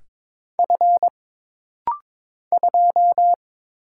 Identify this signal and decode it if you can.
Morse code, heard as a keyed tone, about 22 wpm, 700 Hz. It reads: F 2
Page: This rapid-fire timing is geared to push your ICR (Instant Character Recognition) skills.